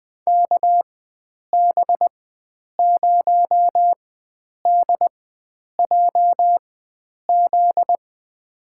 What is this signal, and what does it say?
Morse code, heard as a keyed tone, about 20 wpm, spaced as for 9 wpm, 700 Hz. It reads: KB0DJZ
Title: USA Call Signs - 6 Characters - 4x Character Spacing - 20wpm